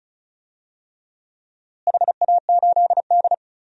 Morse code, heard as a keyed tone, about 36 wpm, 700 Hz. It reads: HA8D
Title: DXCC-100 Call Signs - 4 Characters - Repeated 3x - 35wpm